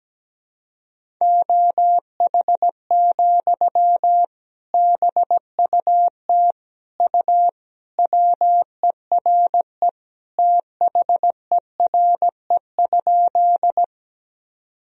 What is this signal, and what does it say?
Morse code, heard as a keyed tone, about 17 wpm, 700 Hz. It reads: OH, BUT U WERE THERE?
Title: Sentences from Top 500 Words and Abbreviations 17wpm